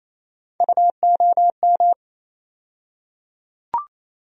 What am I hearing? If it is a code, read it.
Morse code, heard as a keyed tone, about 27 wpm, 700 Hz. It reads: UOM
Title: USA Call Sign Postfixes - 3 Letters 28wpm